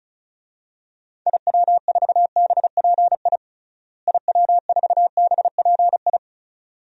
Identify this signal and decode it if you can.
Morse code, heard as a keyed tone, about 36 wpm, 700 Hz. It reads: IW4BPI IW4BPI
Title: DXCC-100 Call Signs - 6 Characters - Repeated 2x - 35wpm